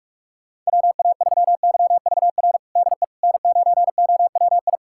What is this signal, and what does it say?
Morse code, heard as a keyed tone, about 41 wpm, 700 Hz. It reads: WA3YVR DE N9OWI
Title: USA Callsign DE Callsign 45wpm